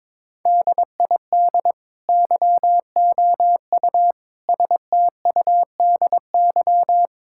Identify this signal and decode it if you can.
Morse code, heard as a keyed tone, about 23 wpm, 700 Hz. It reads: DID YOU STUDY
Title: Sentences from Top 1000 Words - Review Part 6 - 22wpm